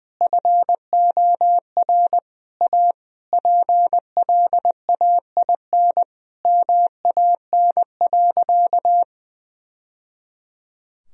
Morse code, heard as a keyed tone, about 20 wpm, 700 Hz. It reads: FOR A PLAIN MAN.
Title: Sentences from Top 1000 Words - Review Part 2 - 20wpm